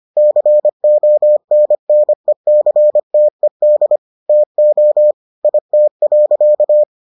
Morse code, heard as a keyed tone, about 25 wpm, 600 Hz. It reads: CONNECTED TO IT.